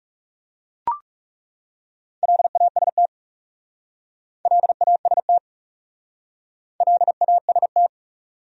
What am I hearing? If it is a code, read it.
Morse code, heard as a keyed tone, about 39 wpm, 700 Hz. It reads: LAST LAST LAST